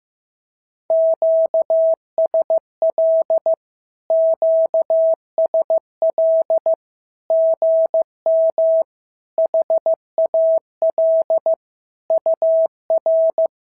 Morse code, heard as a keyed tone, about 15 wpm, 650 Hz. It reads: QSL QSL GM HAL UR
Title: SOTA QSO Practice 15wpm